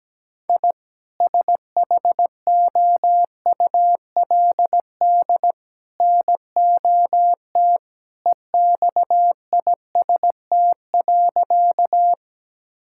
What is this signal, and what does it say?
Morse code, heard as a keyed tone, about 17 wpm, 700 Hz. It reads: I SHOULD NOT EXIST.
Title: Sentences from Top 1000 Words - Review Part 5 - 17wpm